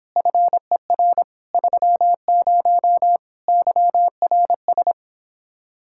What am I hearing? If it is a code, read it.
Morse code, heard as a keyed tone, about 29 wpm, 700 Hz. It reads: FER 30 YRS